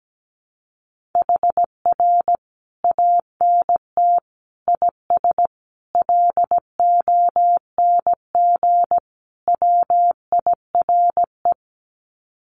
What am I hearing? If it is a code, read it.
Morse code, heard as a keyed tone, about 17 wpm, 700 Hz. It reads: HR ANT IS LONG WIRE
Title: Common QSO Phrases 17wpm